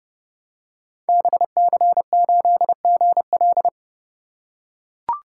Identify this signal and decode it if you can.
Morse code, heard as a keyed tone, about 29 wpm, 700 Hz. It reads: BC8GL
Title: Code Groups - 5 Letter:Numbers - 30wpm